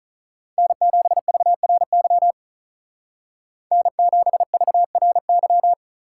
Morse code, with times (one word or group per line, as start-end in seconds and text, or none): 0.6-2.3 N7VRY
3.7-5.7 N7VRY